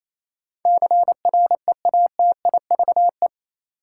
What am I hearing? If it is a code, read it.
Morse code, heard as a keyed tone, about 28 wpm, 700 Hz. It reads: CREATIVE
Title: Sentences from Top 1000 Words - Review Part 5 - New Word Drill - 28wpm